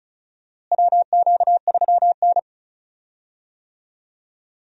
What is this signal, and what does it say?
Morse code, heard as a keyed tone, about 36 wpm, 700 Hz. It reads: WQ3N
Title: USA Call Signs - 4 Characters - Repeated 3x - 35wpm